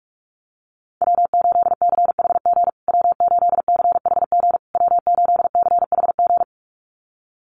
Morse code, heard as a keyed tone, about 44 wpm, 700 Hz. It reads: W8CHG W8CHG W8CHG
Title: USA Call Signs Repeated 3x 45wpm